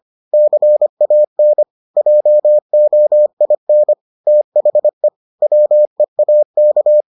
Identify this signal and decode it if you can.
Morse code, heard as a keyed tone, about 25 wpm, 600 Hz. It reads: CAN JOIN THE WEAK